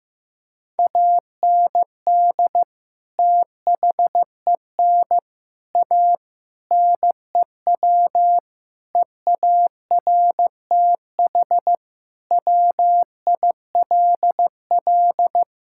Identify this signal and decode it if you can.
Morse code, heard as a keyed tone, about 15 wpm, 700 Hz. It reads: AND THEN A NEW EARTH WILL
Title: Sentences from Top 1000 Words - Review Part 6 - 15wpm